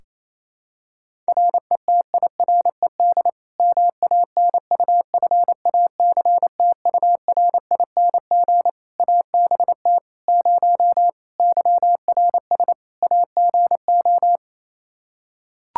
Morse code, heard as a keyed tone, about 28 wpm, 700 Hz. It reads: RETIRED MANUFACTURING ABT 0 YRS AGO
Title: Practice QSO - Exchange 3 - More Info - 28wpm